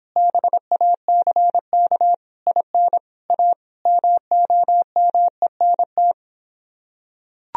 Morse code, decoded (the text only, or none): BACK IN A MOMENT